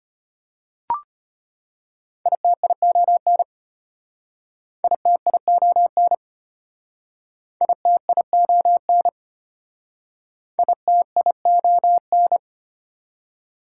Morse code, heard as a keyed tone, about 32 wpm, 690 Hz. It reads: ITION ITION ITION ITION